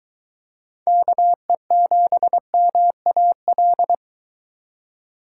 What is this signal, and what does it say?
Morse code, heard as a keyed tone, about 23 wpm, 700 Hz. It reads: KE7MAL